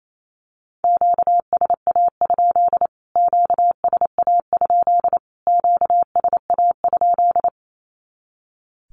Morse code, heard as a keyed tone, about 28 wpm, 700 Hz. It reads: QSA? QSA? QSA?
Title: Q-Codes Repeated 3x 28wpm